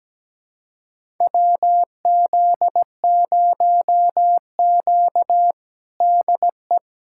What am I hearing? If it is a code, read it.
Morse code, heard as a keyed tone, about 17 wpm, 700 Hz. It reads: WZ0Q DE
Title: USA Callsign DE Callsign - 4 Characters - 17wpm